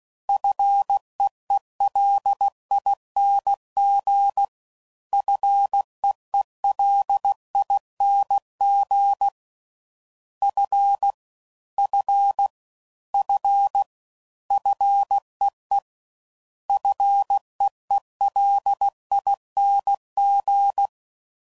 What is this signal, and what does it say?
Morse code, heard as a keyed tone, about 16 wpm, 800 Hz. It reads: FEELING FEELING F F F FEE FEELING
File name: Prefixes / FLs & THs / Single Fs Ls